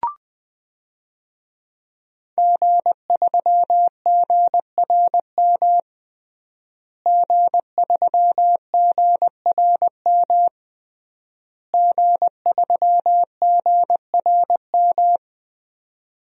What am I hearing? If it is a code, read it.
Morse code, heard as a keyed tone, about 20 wpm, 700 Hz. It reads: G3GRM G3GRM G3GRM
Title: DXCC-100 Call Signs - 5 Characters - Repeated 3x - 20wpm